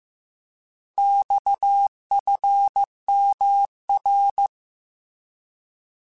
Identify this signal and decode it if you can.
Morse code, heard as a keyed tone, about 15 wpm, 800 Hz. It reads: XFMR
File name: Abreviations / Abreviations mp3s / Abreviations@13